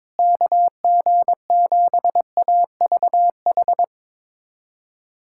Morse code, heard as a keyed tone, about 22 wpm, 700 Hz. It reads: KG7AVH
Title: USA Callsign DE Callsign 22wpm